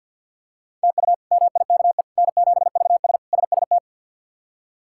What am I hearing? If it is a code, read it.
Morse code, heard as a keyed tone, about 51 wpm, 700 Hz. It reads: TU MIKE N7VS SST
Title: K1USN SST QSO Practice - 50wpm